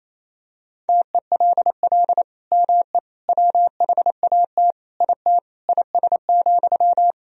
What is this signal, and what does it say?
Morse code, heard as a keyed tone, about 28 wpm, 700 Hz. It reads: TELL ME WHAT IT IS,
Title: Sentences from Top 1000 Words - Review Part 1 - 28wpm